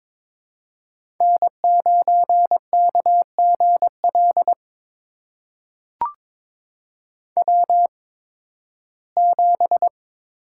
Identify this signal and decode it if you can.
Morse code, heard as a keyed tone, about 22 wpm, spaced as for 14 wpm, 700 Hz. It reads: N9KGL W7
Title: USA Call Signs - 8x Character Spacing - 22wpm